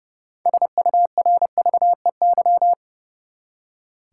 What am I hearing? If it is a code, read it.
Morse code, heard as a keyed tone, about 30 wpm, 700 Hz. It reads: SURVEY
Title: Six Letter Words - 30wpm